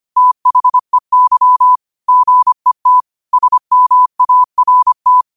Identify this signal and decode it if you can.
Morse code, heard as a keyed tone, about 25 wpm, 1000 Hz. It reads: THEY GET SMART
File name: qotdfiles / 25wpm